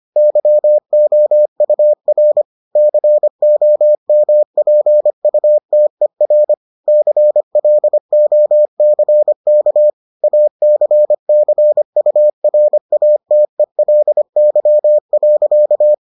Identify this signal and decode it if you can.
Morse code, heard as a keyed tone, about 25 wpm, 600 Hz. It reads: YOUR COMPUTER CLOCK ACCURATELY.